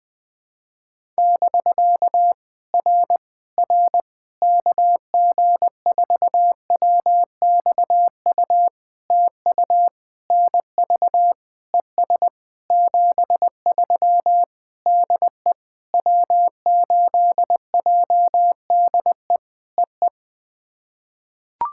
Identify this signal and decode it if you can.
Morse code, heard as a keyed tone, about 20 wpm, 700 Hz. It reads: <BK> R R KG4WXU TU NV ES 73 DE W8JDE EE
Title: POTA QSO Practice - 20wpm